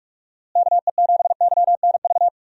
Morse code, heard as a keyed tone, about 45 wpm, 700 Hz. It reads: KE7YNV
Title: USA Call Signs - 6 Characters - Repeated 3x - 45wpm